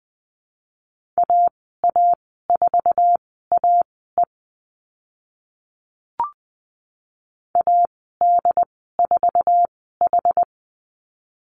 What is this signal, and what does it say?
Morse code, heard as a keyed tone, about 20 wpm, spaced as for 14 wpm, 700 Hz. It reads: AA4AE AD4H
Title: 069 Lesson 12g - Letter d - Call Signs 20wpm